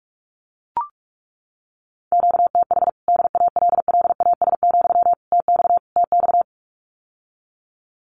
Morse code, heard as a keyed tone, about 44 wpm, 700 Hz. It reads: QTH DALLAS, TX TX
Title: Common QSO Phrases Encore 45wpm